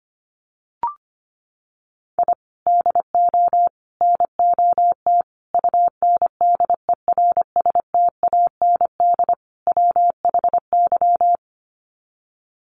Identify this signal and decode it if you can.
Morse code, heard as a keyed tone, about 25 wpm, 700 Hz. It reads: I DO NOT UNDERSTAND WHY
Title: Sets of 5 Words Encore 25wpm